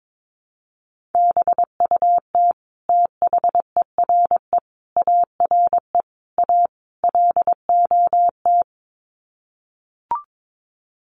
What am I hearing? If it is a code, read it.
Morse code, heard as a keyed tone, about 22 wpm, 700 Hz. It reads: BUT THERE ARE A LOT
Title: Sets of 5 Words Encore2 22wpm